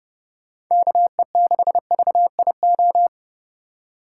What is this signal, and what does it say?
Morse code, heard as a keyed tone, about 30 wpm, 700 Hz. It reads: KE6VIO
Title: USA Call Signs - 4x Character Spacing - 30wpm